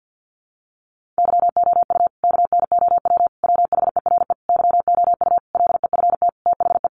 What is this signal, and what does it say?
Morse code, heard as a keyed tone, about 50 wpm, 700 Hz. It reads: YOU KNOW WHERE YOU LEFT THE
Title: Sentences from Top 1500 Words 50wpm